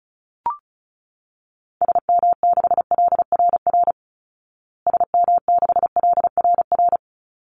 Morse code, heard as a keyed tone, about 34 wpm, 700 Hz. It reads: SM6LRR SM6LRR
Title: Call Signs Repeated 3x - 35wpm